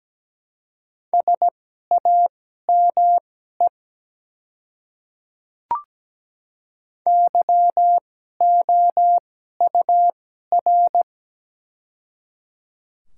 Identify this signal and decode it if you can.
Morse code, heard as a keyed tone, about 17 wpm, spaced as for 12 wpm, 700 Hz. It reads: SAME YOUR
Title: 2x Character Spacing - Top 300 Words - 17wpm